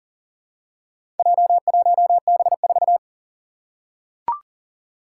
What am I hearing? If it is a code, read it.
Morse code, heard as a keyed tone, about 40 wpm, 700 Hz. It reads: J1B4
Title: Four Challenging Letter-Number Combinations 40wpm